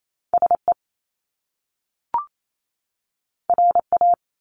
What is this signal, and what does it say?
Morse code, heard as a keyed tone, about 28 wpm, 700 Hz. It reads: SE RA
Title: Top Two Letter Combinations in Words 28wpm